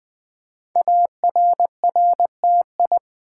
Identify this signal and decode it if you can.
Morse code, heard as a keyed tone, about 20 wpm, 700 Hz. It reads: ARRTI
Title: Code Groups - 5 Letters - 20wpm